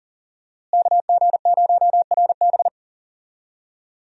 Morse code, heard as a keyed tone, about 40 wpm, 700 Hz. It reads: KG0RB
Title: USA Call Signs - 5 Characters - Repeated 3x - 40wpm